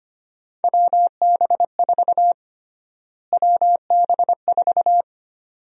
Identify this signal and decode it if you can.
Morse code, heard as a keyed tone, about 25 wpm, 700 Hz. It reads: WB4 WB4
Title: USA Call Sign Prefixes - Repeated 2x - 25wpm